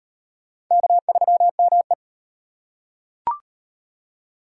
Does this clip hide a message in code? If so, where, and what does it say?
Yes: Morse code from 0.7 to 1.9 s, heard as a keyed tone, about 36 wpm, 700 Hz. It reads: K3ME